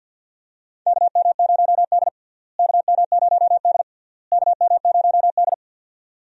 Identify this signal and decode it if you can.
Morse code, heard as a keyed tone, about 52 wpm, 700 Hz. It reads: KM0D KM0D KM0D
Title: USA Call Signs - 4 Characters - Repeated 3x - 50wpm